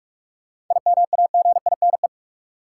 Morse code, heard as a keyed tone, about 46 wpm, 700 Hz. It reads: IMAGINE
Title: Sentences from Top 1400 Words - New Word Drill - 45wpm